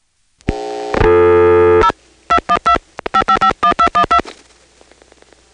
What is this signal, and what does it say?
DTMF, heard as telephone touch tones, tones of about 90 ms. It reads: 0 212 5551212
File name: phonetrips / PrinceWmCoVA5HQ